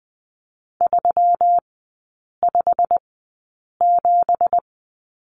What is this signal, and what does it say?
Morse code, heard as a keyed tone, about 20 wpm, spaced as for 8 wpm, 700 Hz. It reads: 357
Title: 212 Lesson 34e - Number 8 - Three Characters 20wpm